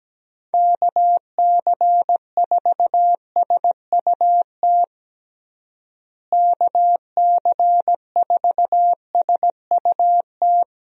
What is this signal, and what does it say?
Morse code, heard as a keyed tone, about 17 wpm, 700 Hz. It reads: KC4SUT KC4SUT
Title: USA Call Signs - 6 Characters - Repeated 2x - 17wpm